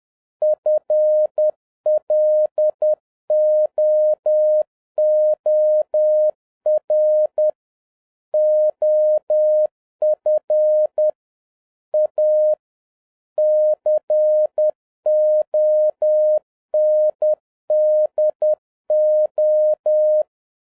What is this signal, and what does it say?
Morse code, heard as a keyed tone, about 10 wpm, 610 Hz. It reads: FLOOR OF A CONDO